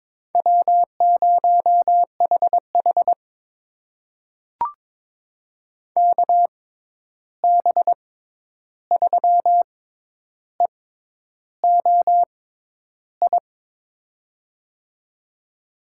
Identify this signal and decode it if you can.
Morse code, heard as a keyed tone, about 22 wpm, spaced as for 10 wpm, 700 Hz. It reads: W0HH KB3EOI
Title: USA Call Signs - 6x Character Spacing - 22wpm